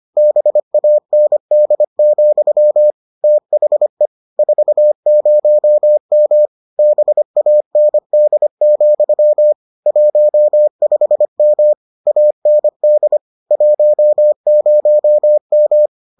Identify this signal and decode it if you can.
Morse code, heard as a keyed tone, about 25 wpm, 600 Hz. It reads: BAND, THE 40M BAND, 15M AND 10M